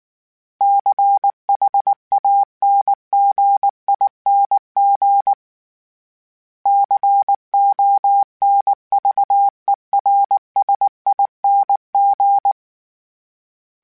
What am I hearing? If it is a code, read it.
Morse code, heard as a keyed tone, about 19 wpm, 800 Hz. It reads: CHANGING CONVERSING
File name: Special INGs / Special INGs-mp3s / Sngl Ing @18